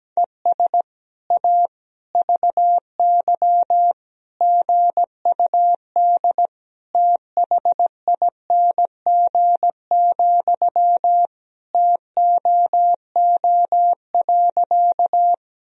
Morse code, heard as a keyed tone, about 17 wpm, 700 Hz. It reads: ES A VY GUD THING, TOO.
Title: Sentences from Top 200 Words and Abbreviations 17wpm